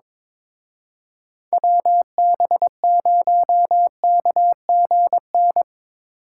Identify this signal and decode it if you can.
Morse code, heard as a keyed tone, about 24 wpm, 700 Hz. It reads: WB0KGN
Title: Speed Racing - USA Call Signs 22wpm